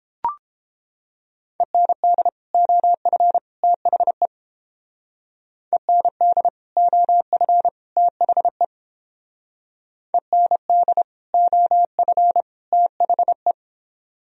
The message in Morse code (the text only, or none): END OF THE END OF THE END OF THE